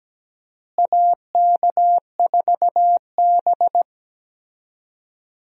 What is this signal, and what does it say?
Morse code, heard as a keyed tone, about 17 wpm, 700 Hz. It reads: AK4B
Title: USA Call Signs - 4 Characters - Repeated 3x - 17wpm